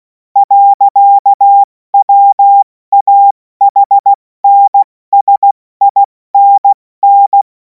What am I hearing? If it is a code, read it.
Morse code, heard as a keyed tone, about 16 wpm, spaced as for 14 wpm, 800 Hz. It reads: . WAHNSINN
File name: Morsekurs / Witze / Witz 11. August 2026 70 F80